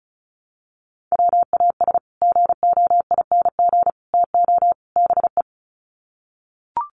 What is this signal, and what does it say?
Morse code, heard as a keyed tone, about 34 wpm, 700 Hz. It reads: WAS GOING TO BE
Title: Sets of 4 Words 35wpm